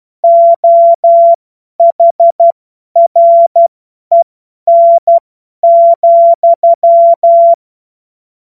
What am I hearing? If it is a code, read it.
Morse code, heard as a keyed tone, about 12 wpm, spaced as for 10 wpm, 680 Hz. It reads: OHREN,
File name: Morsekurs / Witze / Witz 07. Juni 2022 50 F70